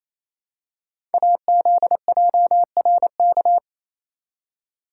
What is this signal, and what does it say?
Morse code, heard as a keyed tone, about 29 wpm, 700 Hz. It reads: AZJRK
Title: Code Groups - 5 Letter:Numbers - 28wpm